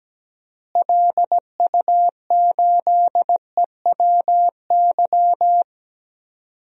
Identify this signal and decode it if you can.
Morse code, heard as a keyed tone, about 17 wpm, 700 Hz. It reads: LU8EWY
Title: DXCC-100 Call Signs - 6 Characters - Repeated 2x - 17wpm